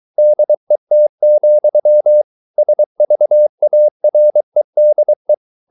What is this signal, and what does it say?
Morse code, heard as a keyed tone, about 23 wpm, 600 Hz. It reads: DET, SVAREDE
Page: Ferie 23 wpm